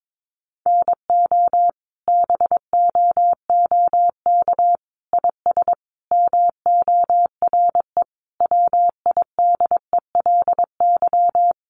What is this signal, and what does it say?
Morse code, heard as a keyed tone, about 22 wpm, 700 Hz. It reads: NO BOOK IS MORE WIDELY
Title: Sentences from Top 1200 Words 22wpm